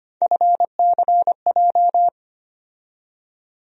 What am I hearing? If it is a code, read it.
Morse code, heard as a keyed tone, about 25 wpm, 700 Hz. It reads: FCJ
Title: USA Call Sign Postfixes - 3 Letters 25wpm